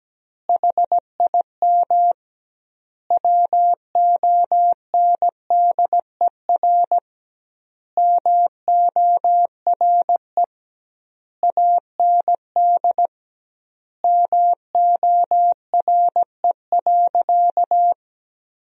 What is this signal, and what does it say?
Morse code, heard as a keyed tone, about 17 wpm, 700 Hz. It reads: HIM WONDER MORE AND MORE.
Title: 2x Word-Spacing - Sentences from Top 500 Words - 17wpm